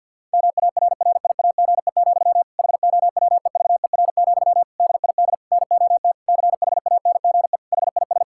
Morse code, heard as a keyed tone, about 52 wpm, 700 Hz. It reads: MARRIAGE, HOWEVER, DID NOT CHANGE HIS
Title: Sentences from Top 1000 Words - Review Part 5 - 50wpm